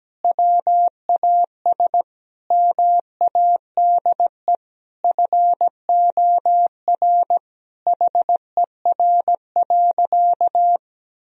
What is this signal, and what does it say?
Morse code, heard as a keyed tone, about 17 wpm, 700 Hz. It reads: WAS MADE FOR HER.
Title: Sentences from Top 1000 Words - Review Part 3 - 17wpm